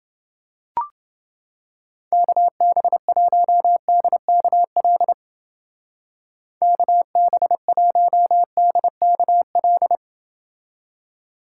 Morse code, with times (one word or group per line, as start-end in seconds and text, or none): 2.1-5.1 KB1DKL
6.6-10.0 KB1DKL